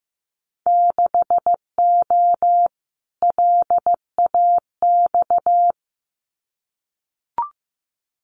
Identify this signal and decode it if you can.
Morse code, heard as a keyed tone, about 15 wpm, 700 Hz. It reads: 6O LAX
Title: Winter Field Day Exchanges Encore 15wpm